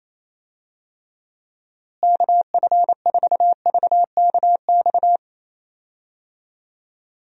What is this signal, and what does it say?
Morse code, heard as a keyed tone, about 27 wpm, 700 Hz. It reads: KF4VKX
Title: USA Call Signs - 6 Characters - Repeated 2x - 28wpm